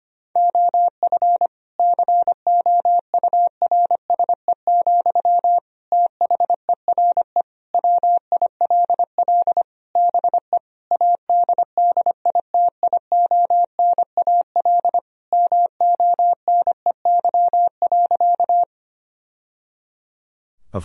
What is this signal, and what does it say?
Morse code, heard as a keyed tone, about 25 wpm, 700 Hz. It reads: OF COURSE, THERE WILL BE ADDITIONAL MONEY.
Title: Sentences from Top 1000 Words - Review Part 4 - 25wpm